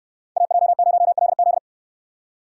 Morse code, heard as a keyed tone, about 68 wpm, 700 Hz. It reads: AJ0LZ